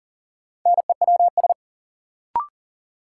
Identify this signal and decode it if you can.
Morse code, heard as a keyed tone, about 40 wpm, 700 Hz. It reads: NEWS
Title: Sentences from Top 800 Words - New Word Drill - 40wpm